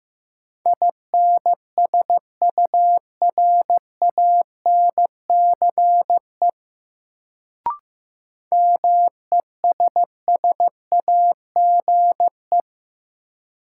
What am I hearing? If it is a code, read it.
Morse code, heard as a keyed tone, about 15 wpm, 700 Hz. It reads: INSURANCE MESSAGE
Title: Sentences from Top 1100 Words - New Word Drill Alt - 15wpm